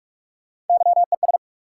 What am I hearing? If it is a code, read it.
Morse code, heard as a keyed tone, about 46 wpm, 700 Hz. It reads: YES
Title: Sentences from Top 500 Words - New Word Drill - 45wpm